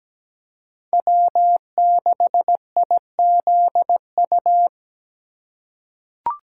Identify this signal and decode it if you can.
Morse code, heard as a keyed tone, about 17 wpm, 700 Hz. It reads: W6IZU
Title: USA Call Signs - 5 Characters - 4x Character Spacing - 17wpm